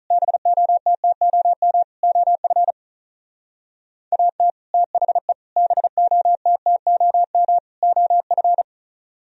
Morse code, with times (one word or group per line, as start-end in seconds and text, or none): 0.1-1.8 BOTTOM
2.0-2.7 OF
4.1-4.5 AT
4.7-5.3 THE
5.6-7.6 BOTTOM
7.8-8.6 OF